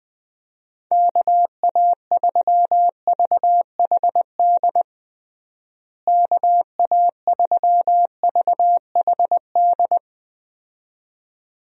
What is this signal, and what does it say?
Morse code, heard as a keyed tone, about 20 wpm, 700 Hz. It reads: KA3VHD KA3VHD
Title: USA Call Signs - 6 Characters - Repeated 2x - 20wpm